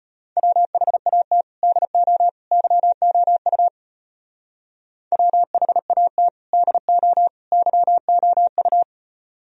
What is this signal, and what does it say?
Morse code, heard as a keyed tone, about 30 wpm, 700 Hz. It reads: WHAT DO YOU WHAT DO YOU
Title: Speed-Racing - Sets of 3 Words 25wpm